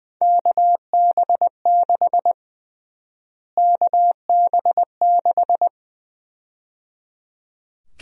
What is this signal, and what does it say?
Morse code, heard as a keyed tone, about 20 wpm, 700 Hz. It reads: KB6 KB6
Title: USA Call Sign Prefixes - Repeated 2x - 20wpm